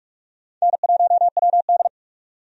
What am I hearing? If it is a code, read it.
Morse code, heard as a keyed tone, about 46 wpm, 700 Hz. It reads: N1WD
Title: USA Call Signs - 4 Characters - 45wpm